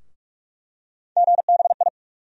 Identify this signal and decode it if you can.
Morse code, heard as a keyed tone, about 45 wpm, 700 Hz. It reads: GBI
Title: USA Call Sign Postfixes - 3 Letters 45wpm